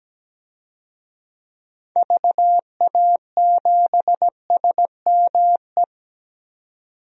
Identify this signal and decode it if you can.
Morse code, heard as a keyed tone, about 17 wpm, 700 Hz. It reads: VA7SME
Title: DXCC-100 Call Signs - 6 Characters - 17wpm